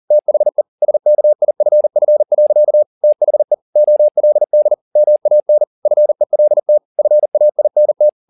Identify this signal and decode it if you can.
Morse code, heard as a keyed tone, about 40 wpm, 600 Hz. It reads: THE SKIFF. THE OLD MAN FELT FAINT
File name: the old man and the sea 2 40wpm